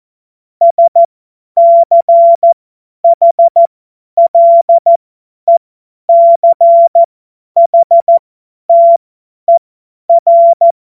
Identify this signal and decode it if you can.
Morse code, heard as a keyed tone, about 14 wpm, spaced as for 10 wpm, 680 Hz. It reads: SCHLECHTER
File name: Morsekurs / Witze / Witz 02. August 2022 50 F70